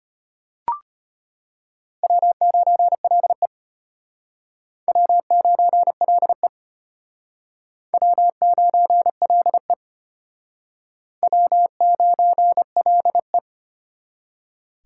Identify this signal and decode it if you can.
Morse code, heard as a keyed tone, about 32 wpm, 700 Hz. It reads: W9LE W9LE W9LE W9LE